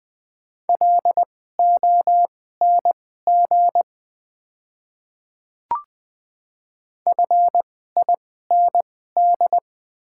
Morse code, heard as a keyed tone, about 20 wpm, spaced as for 14 wpm, 700 Hz. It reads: LONG FIND
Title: 2x Character Spacing - Top 300 Words - 20wpm